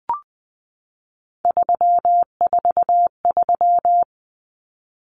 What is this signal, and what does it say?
Morse code, heard as a keyed tone, about 20 wpm, 700 Hz. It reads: 343
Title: Common Cut Numbers - 3-Digit 20wpm